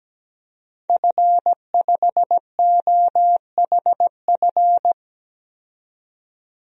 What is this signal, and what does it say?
Morse code, heard as a keyed tone, about 17 wpm, 700 Hz. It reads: F5OHF
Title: DXCC-100 Call Signs - 5 Characters - 17wpm